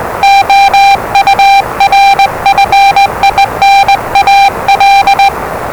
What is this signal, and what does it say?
Morse code, heard as a keyed tone, about 20 wpm, 790 Hz, with noise. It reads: OUR FINAL
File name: WellingtonRadioZlwCloseDownOnHfAndMf / 01-Zlw 4277khz 417.5 qrt until 0925z